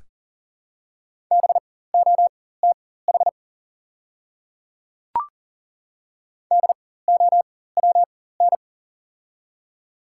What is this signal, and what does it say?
Morse code, heard as a keyed tone, about 40 wpm, spaced as for 19 wpm, 700 Hz. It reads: BOTH DOWN
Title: 4x Character Spacing - Top 300 Words - 40wpm